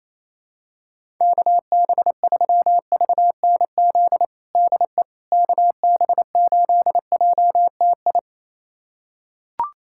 Morse code, heard as a keyed tone, about 28 wpm, 700 Hz. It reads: KB3VNZ DE KB8JTI
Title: USA Callsign DE Callsign 28wpm